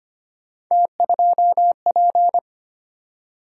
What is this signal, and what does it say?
Morse code, heard as a keyed tone, about 25 wpm, 700 Hz. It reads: T2P
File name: Three Letter-Number Combinations - 25wpm